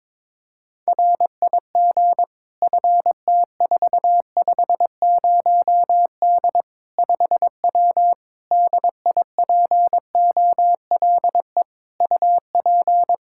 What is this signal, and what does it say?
Morse code, heard as a keyed tone, about 22 wpm, 700 Hz. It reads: RIG FT450D 5W DIPOLE UP
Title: Practice QSO - Exchange 2 - Info - 22wpm